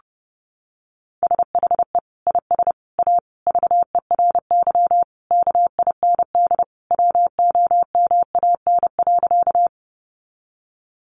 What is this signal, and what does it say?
Morse code, heard as a keyed tone, about 30 wpm, 700 Hz. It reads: SHE IS A VERY KIND WOMAN.
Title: Sentences from Top 1100 Words 30wpm